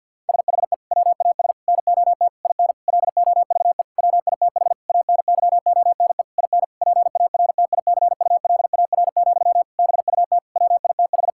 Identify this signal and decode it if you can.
Morse code, heard as a keyed tone, about 50 wpm, 700 Hz. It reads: SHE WAS NOT IN LOVE WITH ANYONE IN PARTICULAR, BUT WITH
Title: Sentences from Top 1000 Words - Review Part 5 - 50wpm